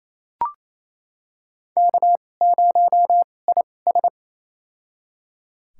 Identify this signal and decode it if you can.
Morse code, heard as a keyed tone, about 28 wpm, spaced as for 20 wpm, 700 Hz. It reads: K0IS
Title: USA Call Signs - 4 Characters - 2x Character Spacing - 28wpm